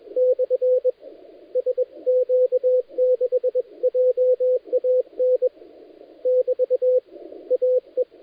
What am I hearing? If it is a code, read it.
Morse code, heard as a keyed tone, about 21 wpm, 510 Hz, with radio noise. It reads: / SQ6JAN <BT> AE
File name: dlazaawansowanych / nagraniazpasma / titawaNETcontrol